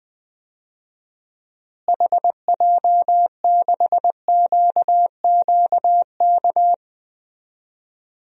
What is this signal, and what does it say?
Morse code, heard as a keyed tone, about 21 wpm, 700 Hz. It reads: HJ6QQK
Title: DXCC-100 Call Signs - 6 Characters - Repeated 3x - 20wpm